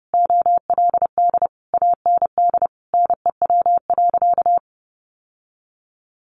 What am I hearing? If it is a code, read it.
Morse code, heard as a keyed tone, about 30 wpm, 700 Hz. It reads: OLD AND NEW.
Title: Sentences from Top 1500 Words 30wpm